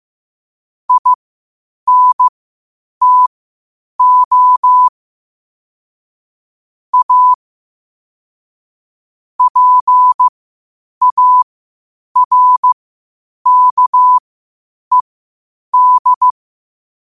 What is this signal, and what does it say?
Morse code, heard as a keyed tone, about 15 wpm, spaced as for 8 wpm, 1000 Hz. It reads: INTO A PARKED
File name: qotdfiles / 8wpm